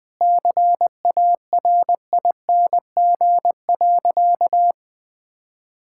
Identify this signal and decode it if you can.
Morse code, heard as a keyed tone, about 20 wpm, 700 Hz. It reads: CARING.
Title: Sentences with 2-Word Phrases - 20wpm